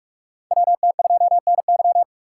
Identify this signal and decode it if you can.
Morse code, heard as a keyed tone, about 43 wpm, 700 Hz. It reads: WT2NY
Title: USA Call Signs - 5 Characters - 6x Character Spacing - 45wpm